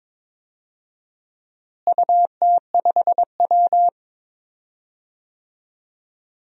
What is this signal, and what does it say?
Morse code, heard as a keyed tone, about 22 wpm, 700 Hz. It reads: UT5W
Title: DXCC-100 Call Signs - 4 Characters - Repeated 2x - 22wpm